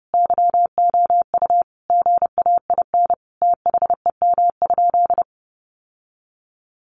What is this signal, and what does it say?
Morse code, heard as a keyed tone, about 30 wpm, 700 Hz. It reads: YOU GAIN THEM?
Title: Sentences from Top 1000 Words 30wpm